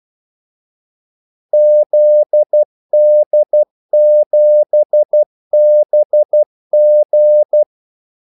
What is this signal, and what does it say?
Morse code, heard as a keyed tone, about 12 wpm, 600 Hz. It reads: ZD7BG
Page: 2 ekstra ordmellemrum mellem kaldesignalerne